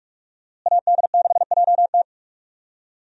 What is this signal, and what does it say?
Morse code, heard as a keyed tone, about 46 wpm, 700 Hz. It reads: AD6JT
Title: USA Call Signs - 5 Characters - Repeated 2x - 45wpm